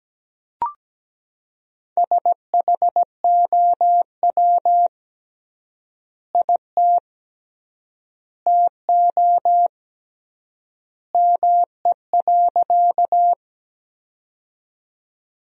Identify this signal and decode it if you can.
Morse code, heard as a keyed tone, about 17 wpm, 700 Hz. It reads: SHOW IT TO ME.
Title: 3x Word-Spacing - Sentences from Top 200 Words - 17wpm